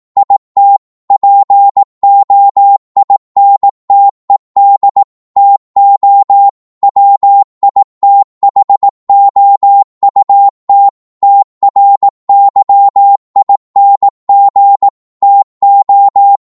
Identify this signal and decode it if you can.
Morse code, heard as a keyed tone, about 18 wpm, 800 Hz. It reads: IT POINTED TO WITHOUT TRYING TO